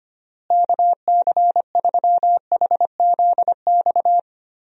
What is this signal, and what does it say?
Morse code, heard as a keyed tone, about 25 wpm, 700 Hz. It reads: KC3HZX
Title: USA Call Signs - 6 Characters - Repeated 3x - 25wpm